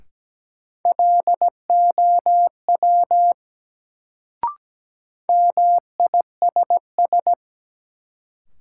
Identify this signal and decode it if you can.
Morse code, heard as a keyed tone, about 17 wpm, 700 Hz. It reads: LOW MISS
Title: Sentences from Top 500 Words - New Word Drill - 17wpm